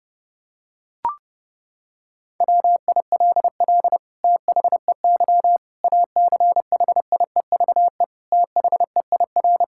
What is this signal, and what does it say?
Morse code, heard as a keyed tone, about 30 wpm, 700 Hz. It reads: WILL THEY ACHIEVE THEIR
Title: Sentences from Top 1000 Words - Review Part 6 - 30wpm